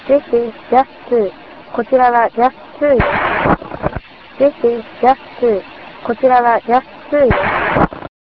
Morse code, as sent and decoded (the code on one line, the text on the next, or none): none